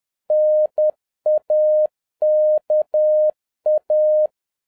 Morse code, heard as a keyed tone, about 10 wpm, 610 Hz. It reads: NAKA
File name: 2011-10-mp3 / n11-MEETING WITH GOOD FRIENDS 10wpm